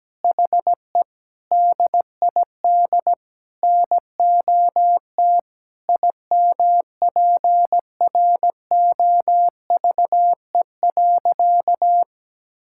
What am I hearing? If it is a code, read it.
Morse code, heard as a keyed tone, about 17 wpm, 700 Hz. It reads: HE DID NOT IMPROVE.
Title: Sentences from Top 1000 Words - Review Part 5 - 17wpm